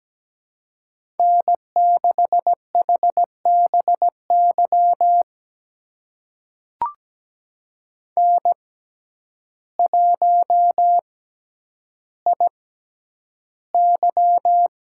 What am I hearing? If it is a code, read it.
Morse code, heard as a keyed tone, about 17 wpm, spaced as for 9 wpm, 700 Hz. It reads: N6HBY N1IY
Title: USA Call Signs - 6x Character Spacing - 17wpm